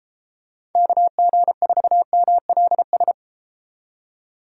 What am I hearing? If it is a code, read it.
Morse code, heard as a keyed tone, about 33 wpm, 700 Hz. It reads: KG4MLS